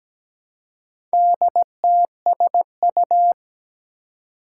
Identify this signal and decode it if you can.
Morse code, heard as a keyed tone, about 17 wpm, 700 Hz. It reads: DTSU
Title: Four Challenging Letter-Number Combinations 17wpm